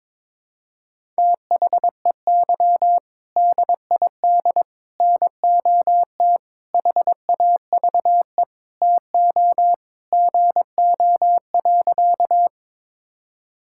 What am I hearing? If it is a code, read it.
Morse code, heard as a keyed tone, about 22 wpm, 700 Hz. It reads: THEY DID NOT HAVE TO GO.
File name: practice / 22wpm / Sentences with 4-Word Phrases - 22wpm